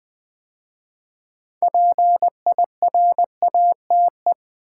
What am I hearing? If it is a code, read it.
Morse code, heard as a keyed tone, about 20 wpm, 700 Hz. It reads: PIRATE
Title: 6 Letter Words - First 3 Letters Given - 20wpm